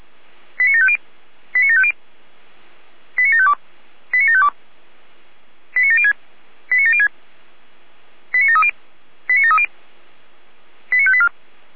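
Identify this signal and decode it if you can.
Five-tone selective call (ZVEI), heard as quick runs of five tones, tones of about 70 ms. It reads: 89750 89750 89752 89752 89706 89706 89730 89730 89584
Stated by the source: Five-tone ZVEI selective call received on an SDR.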